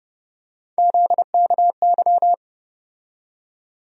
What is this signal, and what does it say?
Morse code, heard as a keyed tone, about 30 wpm, 700 Hz. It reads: ZKY
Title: USA Call Sign Postfixes - 3 Letters 30wpm